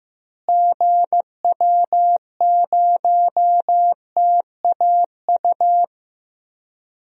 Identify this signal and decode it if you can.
Morse code, heard as a keyed tone, about 15 wpm, 700 Hz. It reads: GW0TAU
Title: DXCC-100 Call Signs - 6 Characters - Repeated 3x - 15wpm